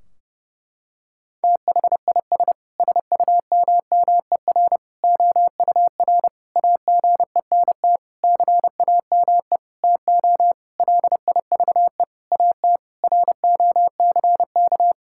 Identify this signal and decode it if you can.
Morse code, heard as a keyed tone, about 32 wpm, 700 Hz. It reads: THIS SUMMER OUR AGENT CAME TO LIVE AT ROCK
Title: Sentences from Top 1600 Words 30wpm